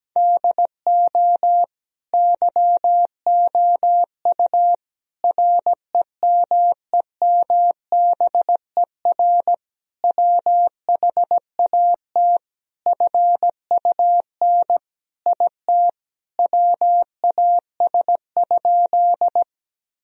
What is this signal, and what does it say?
Morse code, heard as a keyed tone, about 17 wpm, 700 Hz. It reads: DO YOU REMEMBER WHAT FUN IT WAS?
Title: Sentences from Top 1000 Words - Review Part 4 - 17wpm